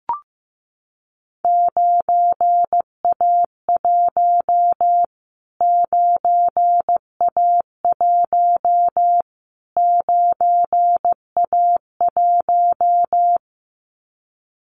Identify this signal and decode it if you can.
Morse code, heard as a keyed tone, about 15 wpm, 700 Hz. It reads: 9A1 9A1 9A1
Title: Callsign Prefixes for Top 25 DXCC - Repeated 3x - 15wpm